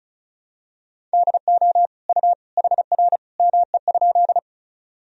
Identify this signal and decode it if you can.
Morse code, heard as a keyed tone, about 36 wpm, 700 Hz. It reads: DO U HR ME?
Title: Sentences from Top 300 Words and Abbreviations 35wpm